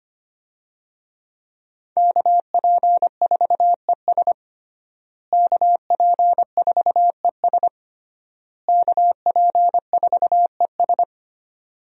Morse code, heard as a keyed tone, about 25 wpm, 700 Hz. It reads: KP4ES KP4ES KP4ES
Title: USA Call Signs - 5 Characters - Repeated 3x - 25wpm